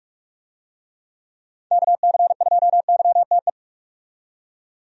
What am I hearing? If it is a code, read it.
Morse code, heard as a keyed tone, about 41 wpm, 700 Hz. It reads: KC2YTE
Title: USA Call Signs - 6 Characters - 45wpm